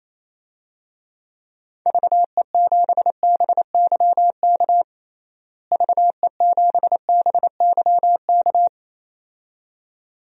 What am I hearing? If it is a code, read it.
Morse code, heard as a keyed tone, about 28 wpm, 700 Hz. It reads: VE7BYK VE7BYK
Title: DXCC-100 Call Signs - 6 Characters - Repeated 2x - 28wpm